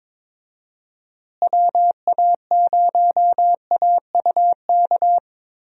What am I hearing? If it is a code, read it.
Morse code, heard as a keyed tone, about 22 wpm, 700 Hz. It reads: WA0AUK
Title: USA Call Signs - 6 Characters - Repeated 2x - 22wpm